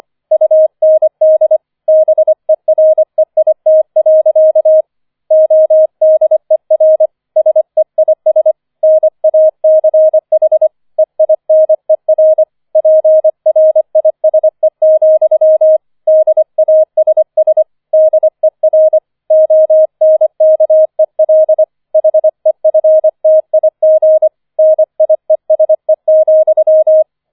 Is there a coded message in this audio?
Yes: Morse code from 0.3 to 27.0 s, heard as a keyed tone, about 25 wpm, 630 Hz. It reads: UND BEREIT. ODER SEIS NACH EINER PRISE, DASS DER ONKEL HEFTIG NIESE,